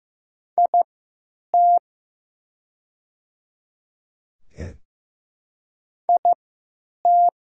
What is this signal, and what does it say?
Morse code, heard as a keyed tone, about 15 wpm, spaced as for 9 wpm, 700 Hz. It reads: IT IT
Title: 3x Character Spacing - Top 100 Words - 15wpm